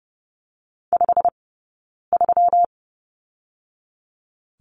Morse code, heard as a keyed tone, about 30 wpm, 700 Hz. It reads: 5 3
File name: Lesson 20d - Number 3 - Two Characters 30wpm